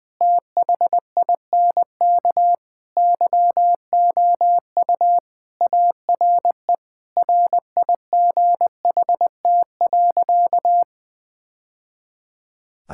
Morse code, heard as a keyed tone, about 20 wpm, 700 Hz. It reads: THINK YOU ARE RIGHT.
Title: Sentences with 3-Word Phrases - 20wpm